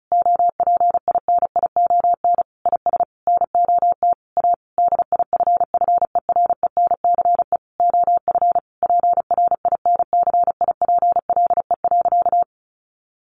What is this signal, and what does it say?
Morse code, heard as a keyed tone, about 34 wpm, 700 Hz. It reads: OPINION IS NOT A DIFFERENCE OF PRINCIPLE.
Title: Sentences from Top 1400 Words 35wpm